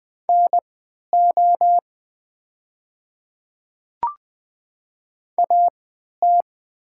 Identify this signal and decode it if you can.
Morse code, heard as a keyed tone, about 20 wpm, spaced as for 11 wpm, 700 Hz. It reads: NO AT
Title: 3x Character Spacing - Top 100 Words - 20wpm